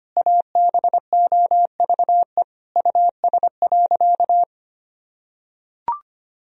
Morse code, heard as a keyed tone, about 25 wpm, 700 Hz. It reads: ABOVE US.
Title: Sentences from Top 1000 Words - Review Part 2 - 25wpm